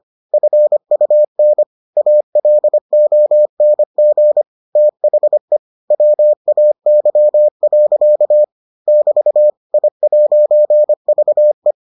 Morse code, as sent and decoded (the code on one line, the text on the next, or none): ..-. ..- -. / .- .-.. --- -. --. / - .... / .-- .- -.-- .-.-.- / -...- / .. .----. ...- .
FUN ALONG THE WAY. <BT> I'VE